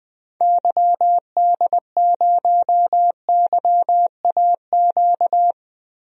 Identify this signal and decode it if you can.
Morse code, heard as a keyed tone, about 20 wpm, 700 Hz. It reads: YD0YAQ
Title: DXCC-100 Call Signs - 6 Characters - 20wpm